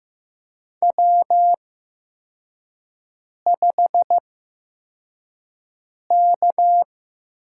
Morse code, heard as a keyed tone, about 15 wpm, spaced as for 4 wpm, 700 Hz. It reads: W5K
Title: USA Call Signs - 5 Characters - 8x Character Spacing - 15wpm